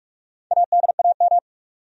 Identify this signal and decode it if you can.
Morse code, heard as a keyed tone, about 44 wpm, 700 Hz. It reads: ADAM
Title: USA Common Names - Max 5 Letters - 45wpm